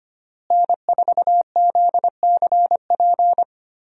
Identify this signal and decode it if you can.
Morse code, heard as a keyed tone, about 25 wpm, 700 Hz. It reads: N4ZCP
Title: USA Call Signs - 5 Characters - Repeated 2x - 25wpm